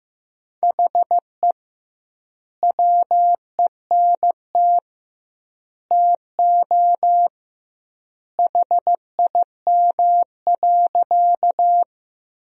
Morse code, heard as a keyed tone, about 15 wpm, 700 Hz. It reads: HE WENT TO HIM.
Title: 2x Word-Spacing - Sentences from Top 200 Words - 15wpm